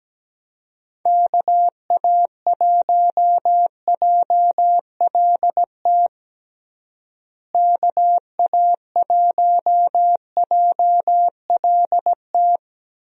Morse code, heard as a keyed tone, about 17 wpm, 700 Hz. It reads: KA1JLT KA1JLT
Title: USA Call Signs - 6 Characters - Repeated 2x - 17wpm